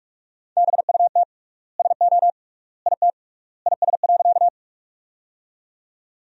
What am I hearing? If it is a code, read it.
Morse code, heard as a keyed tone, about 46 wpm, 700 Hz. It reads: BUT SO IT IS.
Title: 3x Word-Spacing - Sentences from Top 100 Words - 45wpm